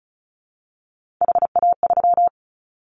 Morse code, heard as a keyed tone, about 34 wpm, 700 Hz. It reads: HA3
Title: Callsign Prefixes for Top 25 DXCC - Repeated 3x - 35wpm